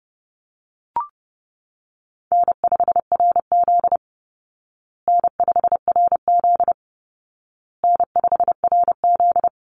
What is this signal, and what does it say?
Morse code, heard as a keyed tone, about 30 wpm, 700 Hz. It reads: N5RZ N5RZ N5RZ
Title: Call Signs Repeated 2x - 30wpm